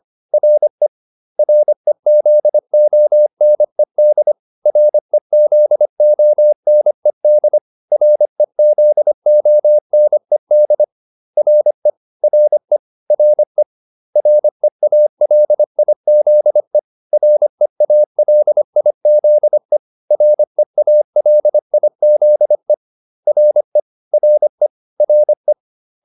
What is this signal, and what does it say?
Morse code, heard as a keyed tone, about 25 wpm, 600 Hz. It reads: RE REZONED REZONED REZONED RE RE RE REALIZE REALIZE REALIZE RE RE RE